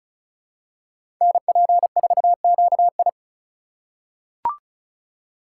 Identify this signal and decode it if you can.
Morse code, heard as a keyed tone, about 34 wpm, 700 Hz. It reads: NP4QI
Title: USA Call Signs - 2x Character Spacing - 35wpm